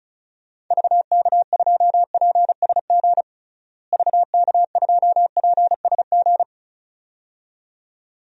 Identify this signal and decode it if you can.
Morse code, heard as a keyed tone, about 34 wpm, 700 Hz. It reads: VK2PSG VK2PSG
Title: DXCC-100 Call Signs - 6 Characters - Repeated 2x - 35wpm